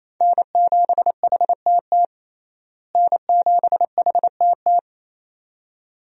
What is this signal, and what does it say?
Morse code, heard as a keyed tone, about 28 wpm, 700 Hz. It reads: N7HTT N7HTT
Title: USA Call Signs - 5 Characters - Repeated 2x - 28wpm